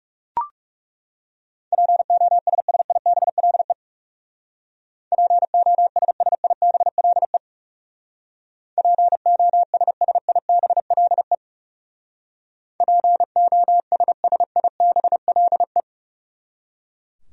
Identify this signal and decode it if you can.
Morse code, heard as a keyed tone, about 38 wpm, 700 Hz. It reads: POSSIBLE POSSIBLE POSSIBLE POSSIBLE